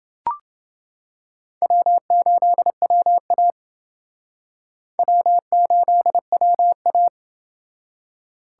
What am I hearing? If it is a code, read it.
Morse code, heard as a keyed tone, about 28 wpm, 700 Hz. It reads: W8WA W8WA